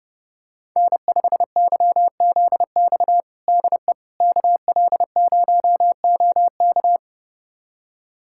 Morse code, heard as a keyed tone, about 30 wpm, 700 Hz. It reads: N5YZX DE KL0OK
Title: USA Callsign DE Callsign - 5 Characters - 30wpm